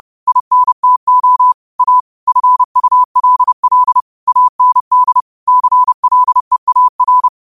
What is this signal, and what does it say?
Morse code, heard as a keyed tone, about 30 wpm, 1000 Hz. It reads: INTO A FULL AND CLEAR